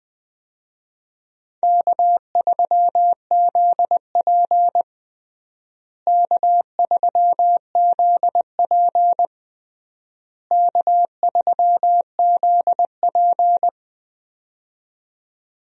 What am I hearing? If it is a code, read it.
Morse code, heard as a keyed tone, about 20 wpm, 700 Hz. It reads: K3ZP K3ZP K3ZP
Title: USA Call Signs - 4 Characters - Repeated 3x - 20wpm